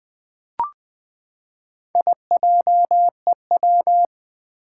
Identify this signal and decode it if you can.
Morse code, heard as a keyed tone, about 20 wpm, 700 Hz. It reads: IJEW
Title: Four Challenging Letter-Number Combinations 20wpm